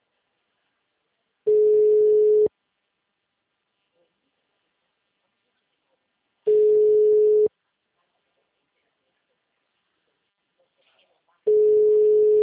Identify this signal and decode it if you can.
Ringback tone (Europe, 425 Hz), with speech.